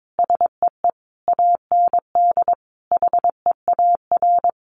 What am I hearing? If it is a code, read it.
Morse code, heard as a keyed tone, about 22 wpm, 700 Hz. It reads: SEE AND HEAR
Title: Sentences from Top 1100 Words 22wpm